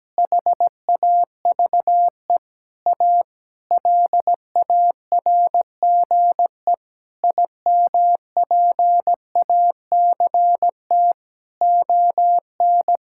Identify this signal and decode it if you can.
Morse code, heard as a keyed tone, about 17 wpm, 700 Hz. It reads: HAVE A LARGE IMPACT ON
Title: Sentences from Top 1000 Words - Review Part 6 - 17wpm